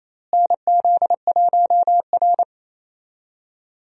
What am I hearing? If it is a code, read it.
Morse code, heard as a keyed tone, about 28 wpm, 700 Hz. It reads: NZ1R
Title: USA Call Signs - 4 Characters - Repeated 3x - 28wpm